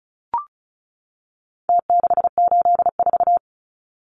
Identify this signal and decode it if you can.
Morse code, heard as a keyed tone, about 34 wpm, 700 Hz. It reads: T684
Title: Numbers - Contest Style - 4-Digit 35wpm